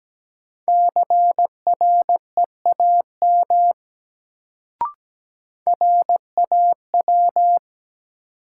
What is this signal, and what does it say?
Morse code, heard as a keyed tone, about 17 wpm, 700 Hz. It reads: CREAM RAW
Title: Sentences from Top 1500 Words - New Word Drill - 17wpm